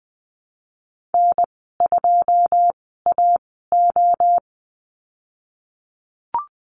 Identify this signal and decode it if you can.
Morse code, heard as a keyed tone, about 20 wpm, spaced as for 15 wpm, 700 Hz. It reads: N2AO
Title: 122 Lesson 20g - Number 3 - Call Signs 20wpm